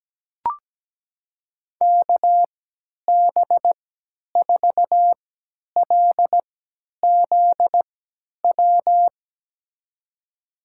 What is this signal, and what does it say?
Morse code, heard as a keyed tone, about 17 wpm, spaced as for 10 wpm, 700 Hz. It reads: KB4LZW
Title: USA Call Signs - 3x Character Spacing - 17wpm